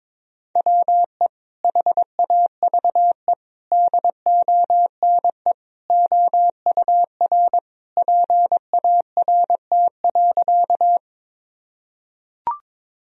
Morse code, heard as a keyed tone, about 22 wpm, 700 Hz. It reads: WE HAVE DONE OUR PART.
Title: Sentences from Top 1000 Words - Review Part 1 - 22wpm